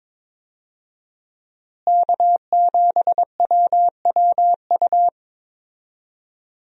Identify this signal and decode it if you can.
Morse code, heard as a keyed tone, about 22 wpm, 700 Hz. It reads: K7WWU
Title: USA Call Signs - 5 Characters - Repeated 2x - 22wpm